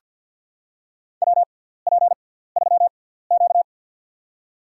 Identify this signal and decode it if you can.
Morse code, heard as a keyed tone, about 51 wpm, spaced as for 17 wpm, 700 Hz. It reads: WP3Q
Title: USA Call Signs - 4 Characters - 6x Character Spacing - 50wpm